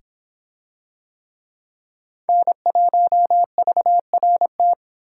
Morse code, heard as a keyed tone, about 26 wpm, 700 Hz. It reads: N1VRT